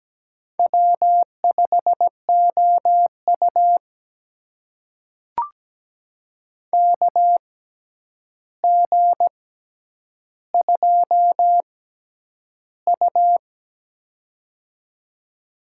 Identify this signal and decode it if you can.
Morse code, heard as a keyed tone, about 17 wpm, spaced as for 9 wpm, 700 Hz. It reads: W5OU KG2U
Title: USA Call Signs - 4 Characters - 6x Character Spacing - 17wpm